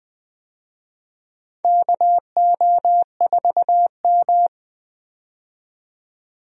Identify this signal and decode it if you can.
Morse code, heard as a keyed tone, about 21 wpm, 700 Hz. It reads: KO4M
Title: USA Call Signs - 4 Characters - 20wpm